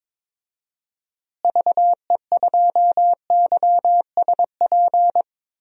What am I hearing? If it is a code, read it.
Morse code, heard as a keyed tone, about 21 wpm, 700 Hz. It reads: VE2YSP
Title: DXCC-100 Call Signs - 6 Characters - Repeated 2x - 22wpm